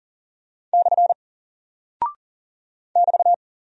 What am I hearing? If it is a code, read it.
Morse code, heard as a keyed tone, about 39 wpm, 700 Hz. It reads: / <BT>
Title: Single Character (Letter:Number:Punctuation:Prosign) 40wpm